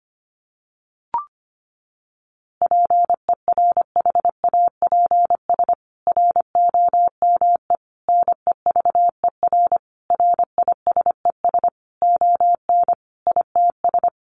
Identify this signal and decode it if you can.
Morse code, heard as a keyed tone, about 25 wpm, 700 Hz. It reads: PERHAPS ROME NEVER RISES ON ITS
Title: A Sound Of Thunder 25wpm